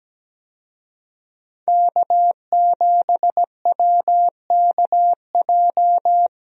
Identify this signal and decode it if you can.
Morse code, heard as a keyed tone, about 17 wpm, 700 Hz. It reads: K7WKJ
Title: USA Call Signs - 5 Characters - 17wpm